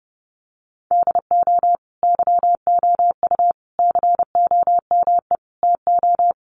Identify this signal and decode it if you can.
Morse code, heard as a keyed tone, about 30 wpm, 700 Hz. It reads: DO YOU COME TO
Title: Sentences from Top 1400 Words 30wpm